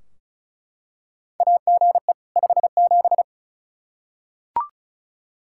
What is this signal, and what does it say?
Morse code, heard as a keyed tone, about 34 wpm, 700 Hz. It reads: AGE 57
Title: Practice QSO - Exchange 3 - More Info - 35wpm